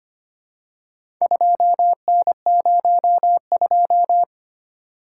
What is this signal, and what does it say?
Morse code, heard as a keyed tone, about 25 wpm, 700 Hz. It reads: 2N02
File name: practice / 25wpm / Four Letter-Number Combinations - 25wpm